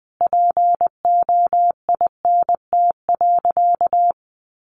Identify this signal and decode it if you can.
Morse code, heard as a keyed tone, about 20 wpm, 700 Hz. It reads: POINT.
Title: Sentences from Top 200 Words 20wpm